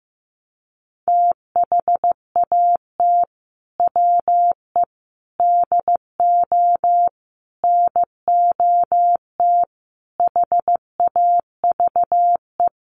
Morse code, heard as a keyed tone, about 15 wpm, 700 Hz. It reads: THAT WE DO NOT HAVE
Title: Sets of 5 Words Encore 15wpm